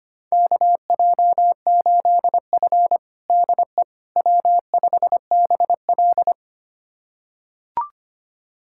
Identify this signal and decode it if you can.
Morse code, heard as a keyed tone, about 25 wpm, 700 Hz. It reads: KJ8F DE W5BL
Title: USA Callsign DE Callsign - 4 Characters - 25wpm